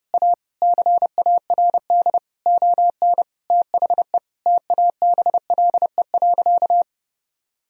Morse code, heard as a keyed tone, about 30 wpm, 700 Hz. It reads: A CARD ON THE TABLE.
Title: Sentences from Top 1000 Words - Review Part 4 - 30wpm